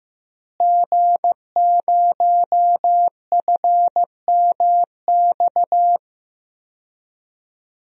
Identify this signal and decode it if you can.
Morse code, heard as a keyed tone, about 15 wpm, 700 Hz. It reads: G0FMX
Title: DXCC-100 Call Signs - 5 Characters - Repeated 2x - 15wpm